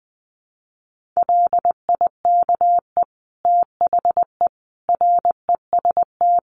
Morse code, heard as a keyed tone, about 20 wpm, 700 Hz. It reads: LIKE THE REST
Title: Sets of 5 Words Encore2 20wpm